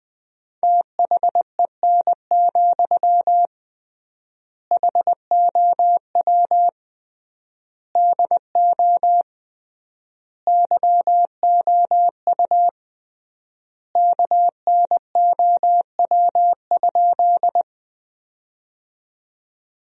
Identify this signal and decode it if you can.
Morse code, heard as a keyed tone, about 20 wpm, 700 Hz. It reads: THEN, HOW DO YOU KNOW?
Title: 3x Word-Spacing - Sentences from Top 100 Words - 20wpm